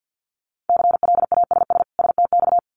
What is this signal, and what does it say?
Morse code, heard as a keyed tone, about 50 wpm, 700 Hz. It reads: CLASS STX
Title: Winter Field Day Exchanges 50wpm